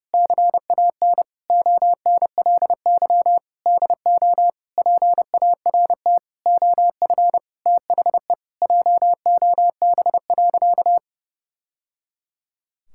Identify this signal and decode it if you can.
Morse code, heard as a keyed tone, about 32 wpm, 700 Hz. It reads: CAN ONLY DO PART OF THE JOB.
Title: Sentences from Top 1000 Words - Review Part 5 - 30wpm